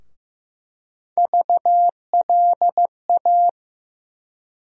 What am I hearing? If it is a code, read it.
Morse code, heard as a keyed tone, about 15 wpm, 700 Hz. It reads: VLA
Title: USA Call Sign Postfixes - 3 Letters 15wpm